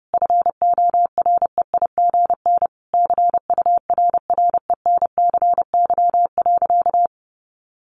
Morse code, heard as a keyed tone, about 30 wpm, 700 Hz. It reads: FOREIGN CURRENCY.
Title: Sentences from Top 1500 Words 30wpm